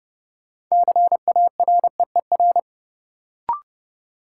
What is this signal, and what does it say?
Morse code, heard as a keyed tone, about 31 wpm, 700 Hz. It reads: CAREER
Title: Sentences from Top 1000 Words - New Word Drill - 30wpm